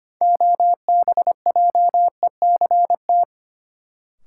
Morse code, heard as a keyed tone, about 25 wpm, 700 Hz. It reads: OBJECT
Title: Sentences from Top 500 Words - New Word Drill - 25wpm